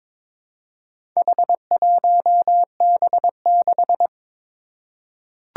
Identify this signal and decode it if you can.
Morse code, heard as a keyed tone, about 22 wpm, 700 Hz. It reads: H1B6
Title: Four Challenging Letter-Number Combinations 22wpm